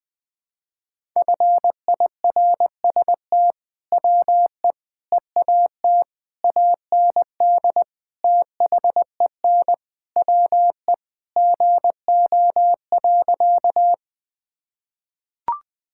Morse code, heard as a keyed tone, about 21 wpm, 700 Hz. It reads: FIRST WE EAT AND THEN WE GO.
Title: Sentences with 2-Word Phrases - 20wpm